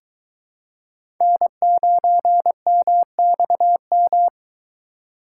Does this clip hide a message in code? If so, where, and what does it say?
Yes: Morse code from 1.2 to 4.3 s, heard as a keyed tone, about 23 wpm, 700 Hz. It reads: N9MXM